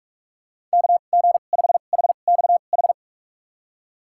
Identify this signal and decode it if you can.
Morse code, heard as a keyed tone, about 46 wpm, spaced as for 32 wpm, 700 Hz. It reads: KG5HXH
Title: USA Call Signs - 6 Characters - 2x Character Spacing - 45wpm